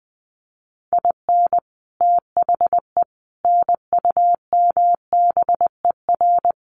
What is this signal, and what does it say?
Morse code, heard as a keyed tone, about 20 wpm, 700 Hz. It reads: IN THE NUMBER
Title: Sets of 4 Words 20wpm